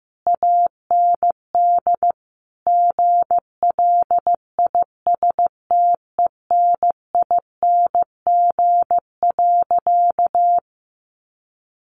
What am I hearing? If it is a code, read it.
Morse code, heard as a keyed tone, about 15 wpm, 700 Hz. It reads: AND GLISTENING.
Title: A Sound Of Thunder 15wpm